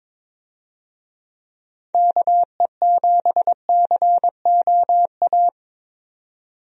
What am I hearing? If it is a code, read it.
Morse code, heard as a keyed tone, about 22 wpm, 700 Hz. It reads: KE7COA
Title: USA Call Signs - 6 Characters - 22wpm